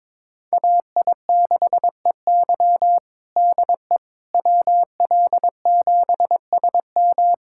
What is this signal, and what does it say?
Morse code, heard as a keyed tone, about 22 wpm, 700 Hz. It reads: AI6EY DE WL7SM
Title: USA Callsign DE Callsign - 5 Characters - 22wpm